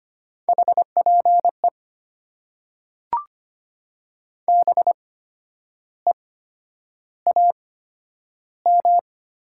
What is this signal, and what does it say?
Morse code, heard as a keyed tone, about 25 wpm, spaced as for 10 wpm, 690 Hz. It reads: HPE BEAM
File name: Top 90 QSO Elements - 8x Character Spacing - 25wpm